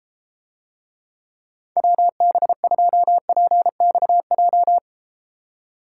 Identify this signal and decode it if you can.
Morse code, heard as a keyed tone, about 32 wpm, 700 Hz. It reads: WB2PXJ